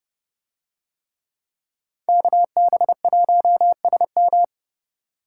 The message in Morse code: KB1SM

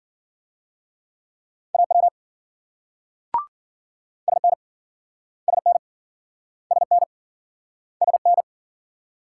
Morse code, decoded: AW SN SN SN SN

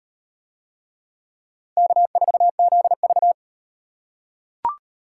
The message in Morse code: K4ZV

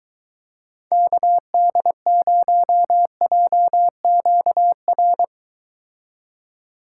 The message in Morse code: KD0JQR

